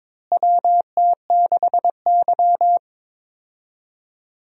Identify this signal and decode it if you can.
Morse code, heard as a keyed tone, about 21 wpm, 700 Hz. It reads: WT6Y
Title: USA Call Signs - 4 Characters - Repeated 2x - 22wpm